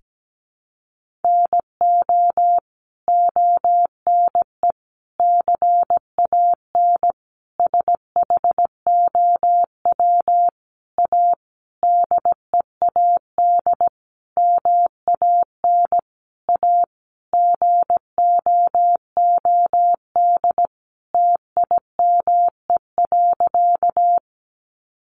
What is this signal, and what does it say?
Morse code, heard as a keyed tone, about 17 wpm, 700 Hz. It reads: NO ONE CAN SHOW A DEAD MAN A GOOD TIME.
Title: Sentences from Top 1500 Words 17wpm